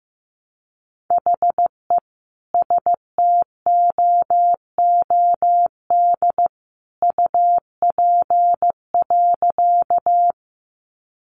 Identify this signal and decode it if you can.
Morse code, heard as a keyed tone, about 15 wpm, 700 Hz. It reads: HE STOOD UP.
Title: Sentences from Top 500 Words 15wpm